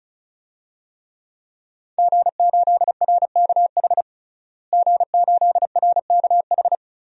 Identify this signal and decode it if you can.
Morse code, heard as a keyed tone, about 36 wpm, 700 Hz. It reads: G8RKH G8RKH
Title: DXCC-100 Call Signs - 5 Characters - Repeated 2x - 35wpm